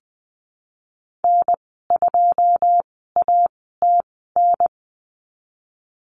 Morse code, heard as a keyed tone, about 19 wpm, spaced as for 14 wpm, 700 Hz. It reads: N2ATN
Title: 088 Lesson 15g - Number 5 - Call Signs 20wpm